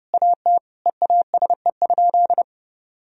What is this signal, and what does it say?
Morse code, heard as a keyed tone, about 30 wpm, 700 Hz. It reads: AT EASE?
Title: Sentences from Top 1000 Words - Review Part 1 - 30wpm